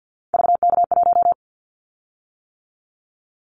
Morse code, heard as a keyed tone, about 50 wpm, 700 Hz. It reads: VK1
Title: Callsign Prefixes for Top 50 DXCC - 50wpm